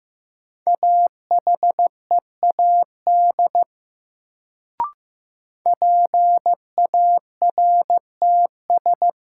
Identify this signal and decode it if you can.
Morse code, heard as a keyed tone, about 15 wpm, 700 Hz. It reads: AHEAD PARTS
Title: Sentences from Top 1000 Words - Review Part 6 - New Word Drill - 15wpm